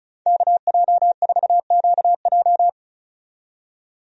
Morse code, heard as a keyed tone, about 34 wpm, 700 Hz. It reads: KJ4QJ
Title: USA Call Signs - 5 Characters - 35wpm